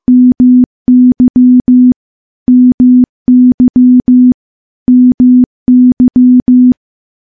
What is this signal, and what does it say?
Morse code, heard as a keyed tone, about 15 wpm, 250 Hz. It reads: MY MY MY
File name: morse-training-15WPM